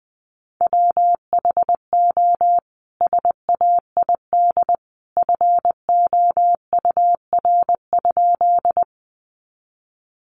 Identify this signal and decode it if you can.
Morse code, heard as a keyed tone, about 20 wpm, 700 Hz. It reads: WHO SAID FOUR?
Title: Sentences from Top 300 Words 20wpm